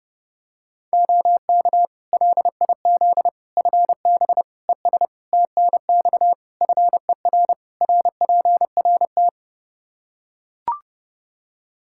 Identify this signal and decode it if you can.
Morse code, heard as a keyed tone, about 32 wpm, 700 Hz. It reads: OK LIZ FB ES TNX FER RPRT
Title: K1USN SST QSO Practice 30wpm